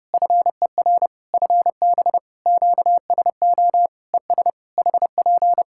Morse code, heard as a keyed tone, about 30 wpm, 700 Hz. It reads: FER FB QSO ES HP
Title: Practice QSO - Exchange 4 - Ending - 30wpm